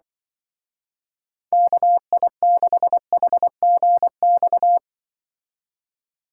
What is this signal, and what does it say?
Morse code, heard as a keyed tone, about 23 wpm, 700 Hz. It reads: KI6HGX
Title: Speed Racing - USA Call Signs 20wpm